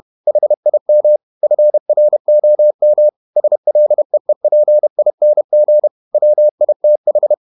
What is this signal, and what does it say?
Morse code, heard as a keyed tone, about 31 wpm, 600 Hz. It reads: HIM FROM SLEEPING WITH
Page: Never 31wpm | CW med Gnister